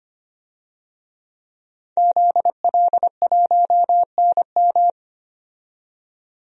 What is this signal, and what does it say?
Morse code, heard as a keyed tone, about 25 wpm, 700 Hz. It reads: ZL1NM
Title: DXCC-100 Call Signs - 5 Characters - 25wpm